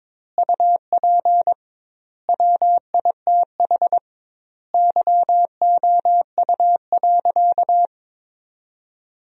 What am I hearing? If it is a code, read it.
Morse code, heard as a keyed tone, about 22 wpm, 700 Hz. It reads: UP WITH YOU.
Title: 2x Word-Spacing - Sentences from Top 300 Words - 22wpm